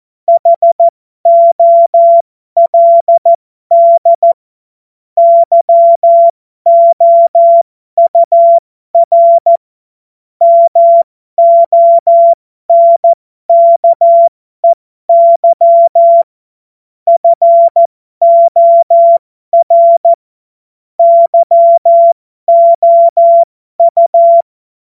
Morse code, heard as a keyed tone, about 14 wpm, spaced as for 12 wpm, 680 Hz. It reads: HOLD YOUR MONKEY FOR YOU
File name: Morsekurs / Witze